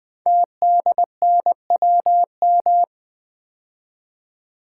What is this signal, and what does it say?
Morse code, heard as a keyed tone, about 20 wpm, 700 Hz. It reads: TDNWM
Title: Code Groups - 5 Letters - 20wpm